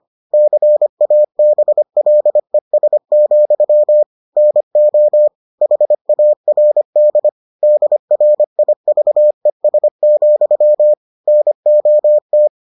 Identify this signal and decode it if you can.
Morse code, heard as a keyed tone, about 25 wpm, 600 Hz. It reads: CABLES, NO HARD DRIVES, NOT